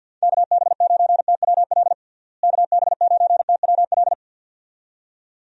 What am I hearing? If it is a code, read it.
Morse code, heard as a keyed tone, about 46 wpm, 700 Hz. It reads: KB9TWL KB9TWL
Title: USA Call Signs - 6 Characters - Repeated 2x - 50wpm